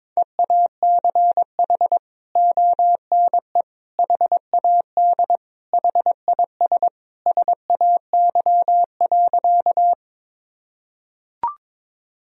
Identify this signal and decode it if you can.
Morse code, heard as a keyed tone, about 21 wpm, 700 Hz. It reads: EACH ONE HAD HIS SAY.
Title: Sentences from Top 1000 Words - Review Part 1 - 22wpm